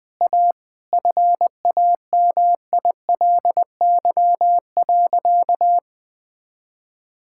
Morse code, heard as a keyed tone, about 21 wpm, 700 Hz. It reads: A FAMILY.
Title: Difficult Sentences with Hints - 20wpm